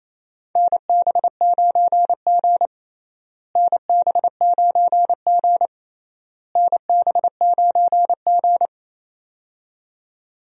Morse code, heard as a keyed tone, about 28 wpm, 700 Hz. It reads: NB9G NB9G NB9G
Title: USA Call Signs - 4 Characters - Repeated 3x - 28wpm